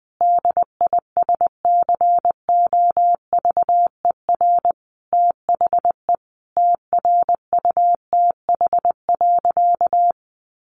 Morse code, heard as a keyed tone, about 20 wpm, 700 Hz. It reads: DISCOVER THE TRUTH.
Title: Sentences from Top 1100 Words 20wpm